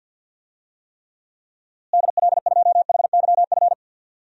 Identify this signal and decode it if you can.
Morse code, heard as a keyed tone, about 50 wpm, 700 Hz. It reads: DL2HYF
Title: DXCC-100 Call Signs - 6 Characters - Repeated 2x - 50wpm